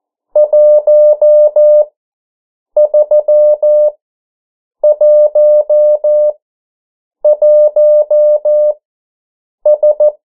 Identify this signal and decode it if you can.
Morse code, heard as a keyed tone, about 14 wpm, spaced as for 7 wpm, 600 Hz. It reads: 1311S
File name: morse mp3 / 09 urok 8